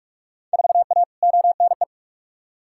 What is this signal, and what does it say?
Morse code, heard as a keyed tone, about 46 wpm, 700 Hz. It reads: 4A ONE
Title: Field Day Exchanges Encore - 45wpm